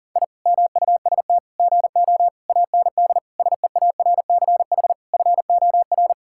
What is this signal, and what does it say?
Morse code, heard as a keyed tone, about 37 wpm, 700 Hz. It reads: I MUST GO AND SEARCH FOR
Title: Sentences from Top 1000 Words - Review Part 5 - 40wpm